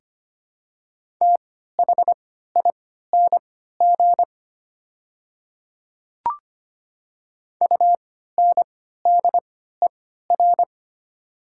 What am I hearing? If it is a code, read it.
Morse code, heard as a keyed tone, about 25 wpm, spaced as for 14 wpm, 690 Hz. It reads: THING UNDER
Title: 3x Character Spacing - Top 300 Words - 25wpm